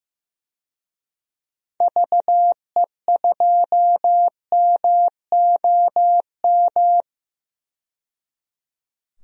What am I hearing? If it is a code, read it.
Morse code, heard as a keyed tone, about 15 wpm, 700 Hz. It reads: VE2MOM
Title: DXCC-100 Call Signs - 6 Characters - 15wpm